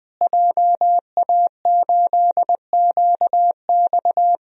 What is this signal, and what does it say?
Morse code, heard as a keyed tone, about 20 wpm, 700 Hz. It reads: JA8QX
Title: DXCC-100 Call Signs - 5 Characters - Repeated 2x - 20wpm